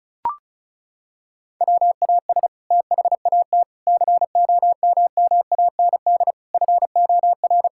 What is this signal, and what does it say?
Morse code, heard as a keyed tone, about 34 wpm, 700 Hz. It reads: WAS THAT COMMAND FOR
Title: Sentences from Top 1600 Words 35wpm